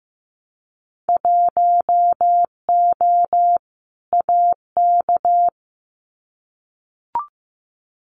Morse code, heard as a keyed tone, about 15 wpm, 700 Hz. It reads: 1O AK
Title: Winter Field Day Exchanges Encore 15wpm